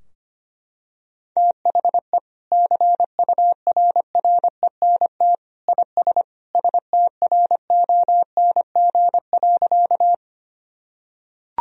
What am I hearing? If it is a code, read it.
Morse code, heard as a keyed tone, about 25 wpm, 700 Hz. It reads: THE CURRENT IS STRONG.
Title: Sentences from Top 1000 Words - Review Part 4 - 25wpm